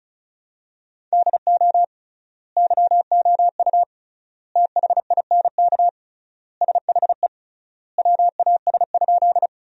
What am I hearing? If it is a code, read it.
Morse code, heard as a keyed tone, about 36 wpm, 700 Hz. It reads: DO YOU THINK SHE WAS?
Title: 3x Word-Spacing - Sentences from Top 200 Words - 35wpm